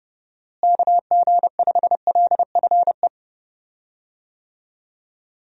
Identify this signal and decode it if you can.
Morse code, heard as a keyed tone, about 30 wpm, 700 Hz. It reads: KG5LFE
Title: USA Call Signs - 6 Characters - Repeated 3x - 30wpm